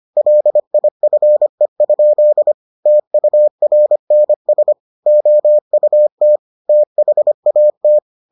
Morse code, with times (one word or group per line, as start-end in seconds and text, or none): 0.2-2.5 LIFE?
2.9-4.7 TURNS
5.1-6.4 OUT
6.7-8.0 THAT